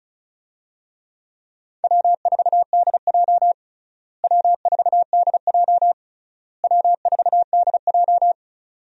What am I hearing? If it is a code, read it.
Morse code, heard as a keyed tone, about 36 wpm, 700 Hz. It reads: W4DJ W4DJ W4DJ
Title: USA Call Signs - 4 Characters - Repeated 3x - 35wpm